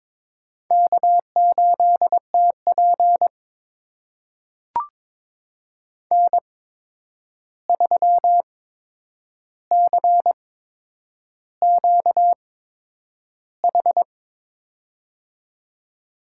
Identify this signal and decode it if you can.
Morse code, heard as a keyed tone, about 22 wpm, spaced as for 9 wpm, 700 Hz. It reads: K8TP N3CQH
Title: USA Call Signs - 8x Character Spacing - 22wpm